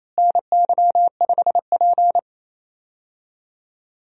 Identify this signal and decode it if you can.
Morse code, heard as a keyed tone, about 28 wpm, 700 Hz. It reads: NY5P
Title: USA Call Signs - 4 Characters - Repeated 2x - 28wpm